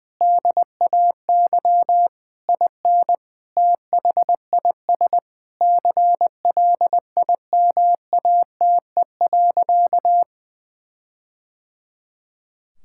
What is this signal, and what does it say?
Morse code, heard as a keyed tone, about 20 wpm, 700 Hz. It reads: DAY IN THIS CLIMATE.
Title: Sentences from Top 1600 Words 20wpm